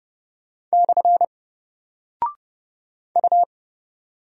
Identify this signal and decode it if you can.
Morse code, heard as a keyed tone, about 30 wpm, 700 Hz. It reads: / U
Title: Single Character (Letter:Number:Punctuation:Prosign) Alt 30wpm